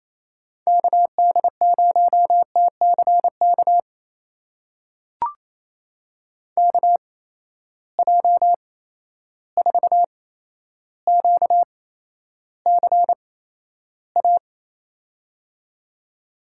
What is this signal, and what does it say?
Morse code, heard as a keyed tone, about 28 wpm, spaced as for 12 wpm, 700 Hz. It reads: KD0TCK KJ4QCA
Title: USA Call Signs - 6 Characters - 8x Character Spacing - 28wpm